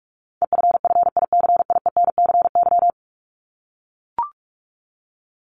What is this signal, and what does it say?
Morse code, heard as a keyed tone, about 44 wpm, 700 Hz. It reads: EFFICIENCY
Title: Top 1501-1750 Words 45wpm